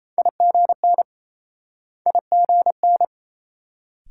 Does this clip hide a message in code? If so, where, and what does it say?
Yes: Morse code from 0.2 to 3.1 s, heard as a keyed tone, about 36 wpm, 700 Hz. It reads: IGN IGN